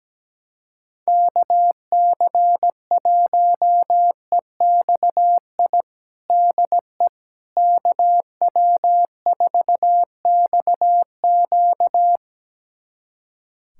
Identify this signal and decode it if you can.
Morse code, heard as a keyed tone, about 17 wpm, 700 Hz. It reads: KC1EXI DE KW4XQ
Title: USA Callsign DE Callsign 17wpm